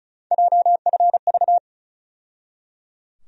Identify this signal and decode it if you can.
Morse code, heard as a keyed tone, about 34 wpm, 700 Hz. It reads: JFV
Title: USA Call Sign Postfixes - 3 Letters 35wpm